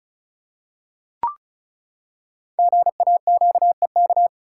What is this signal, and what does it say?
Morse code, heard as a keyed tone, about 36 wpm, 700 Hz. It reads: GAQEK
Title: Code Groups - 5 Letters - 35wpm